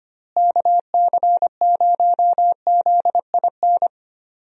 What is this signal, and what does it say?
Morse code, heard as a keyed tone, about 25 wpm, 700 Hz. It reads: KC0ZIN
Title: USA Call Signs - 6 Characters - Repeated 3x - 25wpm